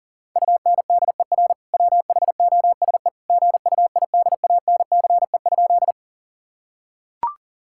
Morse code, heard as a keyed tone, about 40 wpm, 700 Hz. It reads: UNDER WHOSE GUIDANCE?
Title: Sentences from Top 1600 Words 40wpm